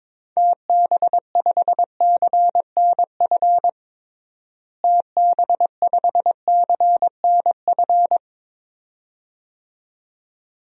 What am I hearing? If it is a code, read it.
Morse code, heard as a keyed tone, about 22 wpm, 700 Hz. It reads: TB5CNF TB5CNF
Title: DXCC-100 Call Signs - 6 Characters - Repeated 2x - 22wpm